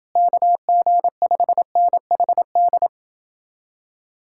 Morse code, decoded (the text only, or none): KG5NHD